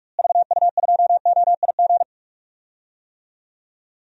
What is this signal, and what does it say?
Morse code, heard as a keyed tone, about 45 wpm, 700 Hz. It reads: VU2OIG
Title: DXCC-100 Call Signs - 6 Characters - Repeated 2x - 45wpm